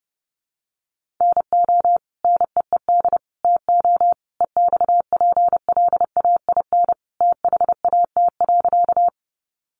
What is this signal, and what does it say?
Morse code, heard as a keyed tone, about 30 wpm, 700 Hz. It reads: NO NEED TO EXPLAIN THAT.
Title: Sentences from Top 1100 Words 30wpm